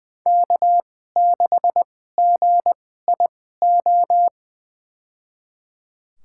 Morse code, heard as a keyed tone, about 20 wpm, spaced as for 15 wpm, 700 Hz. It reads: K6GIO
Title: USA Call Signs - 5 Characters - 2x Character Spacing - 20wpm